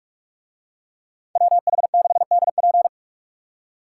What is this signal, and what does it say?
Morse code, heard as a keyed tone, about 44 wpm, 700 Hz. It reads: WH6DP